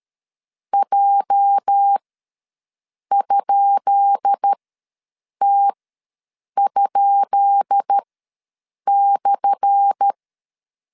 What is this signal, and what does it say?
Morse code, heard as a keyed tone, about 13 wpm, spaced as for 7 wpm, 800 Hz. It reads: J?T?/
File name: LEZIONI / 10 SIGNS / 10B INT ws 12wpm 3527